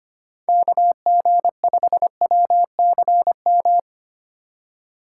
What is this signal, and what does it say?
Morse code, heard as a keyed tone, about 25 wpm, 700 Hz. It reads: KG5WCM
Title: USA Call Signs - 3x Character Spacing - 25wpm